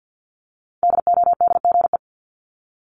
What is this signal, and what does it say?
Morse code, heard as a keyed tone, about 50 wpm, 700 Hz. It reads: DODGE
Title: Multi-letter Phonemes Repeated 3x 50wpm